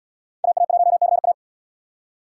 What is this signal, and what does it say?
Morse code, heard as a keyed tone, about 66 wpm, 700 Hz. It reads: AA1CA